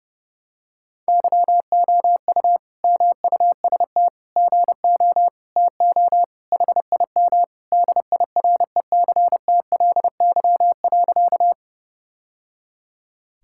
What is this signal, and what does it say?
Morse code, heard as a keyed tone, about 30 wpm, 700 Hz. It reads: YOU MUST GO TO HIM DIRECTLY.
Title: Sentences from Top 1000 Words - Review Part 5 - 30wpm